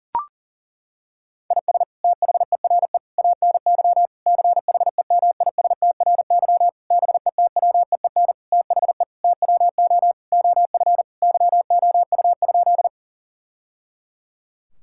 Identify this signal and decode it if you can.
Morse code, heard as a keyed tone, about 37 wpm, 700 Hz. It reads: IS THERE ANY CHEMISTRY BETWEEN THE TWO OF YOU?
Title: Sentences from Top 1000 Words - Review Part 6 - 40wpm